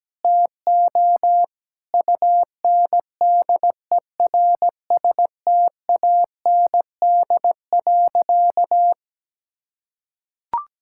Morse code, heard as a keyed tone, about 17 wpm, 700 Hz. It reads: TO UNDERSTAND.
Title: Sentences from Top 1000 Words - Review Part 6 - 17wpm